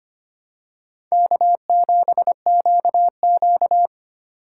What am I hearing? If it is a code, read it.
Morse code, heard as a keyed tone, about 25 wpm, 700 Hz. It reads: K7QQ
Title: USA Call Signs - 4 Characters - 6x Character Spacing - 25wpm